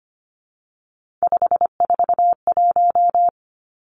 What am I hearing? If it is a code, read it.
Morse code, heard as a keyed tone, about 25 wpm, 700 Hz. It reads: 541
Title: Three Challenging Letter-Number Combinations 25wpm